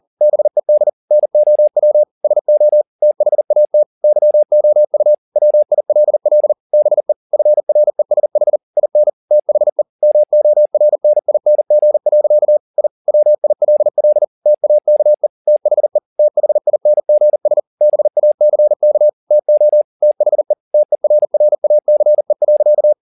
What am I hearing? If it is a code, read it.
Morse code, heard as a keyed tone, about 40 wpm, 600 Hz. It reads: BED NOW SO THAT YOU WILL BE FRESH IN THE MORNING. I WILL TAKE THE THINGS BACK TO THE TERRACE.